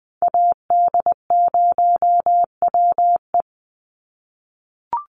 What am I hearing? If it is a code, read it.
Morse code, heard as a keyed tone, about 20 wpm, 700 Hz. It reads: AD0WE
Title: Call Signs - 20wpm